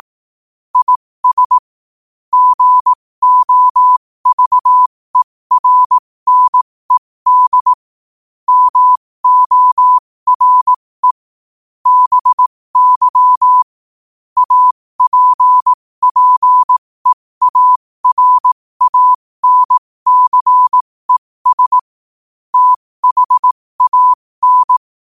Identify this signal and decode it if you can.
Morse code, heard as a keyed tone, about 18 wpm, spaced as for 15 wpm, 1000 Hz. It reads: IS GOVERNED MORE BY APPEARANCES THAN
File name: qotdfiles / 15wpm